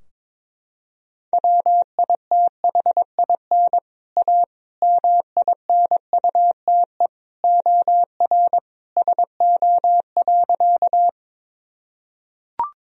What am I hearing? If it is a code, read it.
Morse code, heard as a keyed tone, about 22 wpm, 700 Hz. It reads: WITHIN A MINUTE OR SO.
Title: Sentences from Top 1000 Words - Review Part 4 - 22wpm